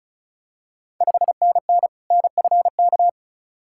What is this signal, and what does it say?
Morse code, heard as a keyed tone, about 36 wpm, 700 Hz. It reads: 5NN NFK
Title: Ontario QSO Party Exchange 35wpm